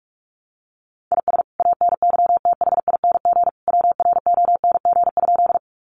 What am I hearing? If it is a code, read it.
Morse code, heard as a keyed tone, about 44 wpm, 700 Hz. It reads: IS ANYTHING WRONG?
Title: Sentences from Top 1000 Words 45wpm